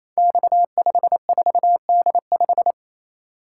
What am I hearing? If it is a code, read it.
Morse code, heard as a keyed tone, about 29 wpm, 700 Hz. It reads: X54D5
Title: Code Groups - 5 Letter:Numbers - 28wpm